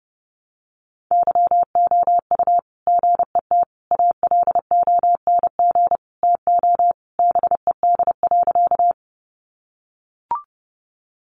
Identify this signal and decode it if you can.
Morse code, heard as a keyed tone, about 30 wpm, 700 Hz. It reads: YOU GET ALONG TO BED.
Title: Sentences from Top 700 Words 30wpm